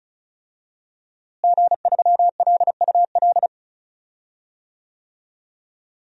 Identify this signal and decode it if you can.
Morse code, heard as a keyed tone, about 36 wpm, 700 Hz. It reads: G3LUL
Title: DXCC-100 Call Signs - 5 Characters - Repeated 3x - 35wpm